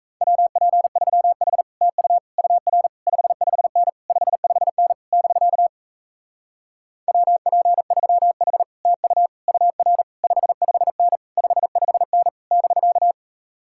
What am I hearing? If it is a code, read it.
Morse code, heard as a keyed tone, about 37 wpm, 700 Hz. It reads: WP3H TU UR 55N 55N <BK> WP3H TU UR 55N 55N <BK>